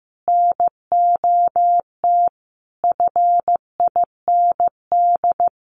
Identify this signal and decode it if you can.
Morse code, heard as a keyed tone, about 15 wpm, 700 Hz. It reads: NOT FIND
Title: Sentences from Top 1400 Words 15wpm